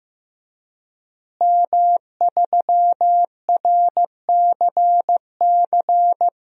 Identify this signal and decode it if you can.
Morse code, heard as a keyed tone, about 15 wpm, 700 Hz. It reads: M3RCC
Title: DXCC-100 Call Signs - 5 Characters - Repeated 2x - 15wpm